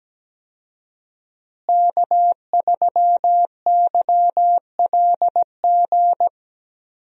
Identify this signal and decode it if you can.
Morse code, heard as a keyed tone, about 17 wpm, 700 Hz. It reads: K3YLG
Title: USA Call Signs - 5 Characters - Repeated 2x - 17wpm